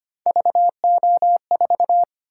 Morse code, heard as a keyed tone, about 25 wpm, 700 Hz. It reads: VO4
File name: Three Letter-Number Combinations - 25wpm